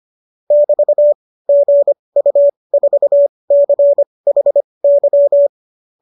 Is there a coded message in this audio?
Yes: Morse code from 0.5 to 5.5 s, heard as a keyed tone, about 25 wpm, spaced as for 20 wpm, 590 Hz. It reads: <BT> GU4CHY